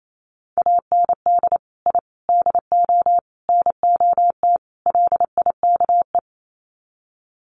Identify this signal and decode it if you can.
Morse code, heard as a keyed tone, about 28 wpm, 700 Hz. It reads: AND I DO NOT LIKE
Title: Sets of 5 Words Encore2 28wpm